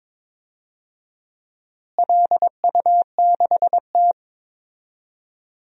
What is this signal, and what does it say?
Morse code, heard as a keyed tone, about 21 wpm, 700 Hz. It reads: LU6T
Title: DXCC-100 Call Signs - 4 Characters - Repeated 2x - 22wpm